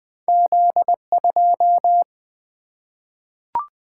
Morse code, heard as a keyed tone, about 21 wpm, 700 Hz. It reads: Z 2
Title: Single-Letter Single-Number 20wpm